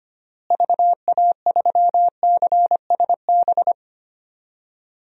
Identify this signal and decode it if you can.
Morse code, heard as a keyed tone, about 25 wpm, 700 Hz. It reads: VA3CSB
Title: DXCC-100 Call Signs - 6 Characters - Repeated 3x - 25wpm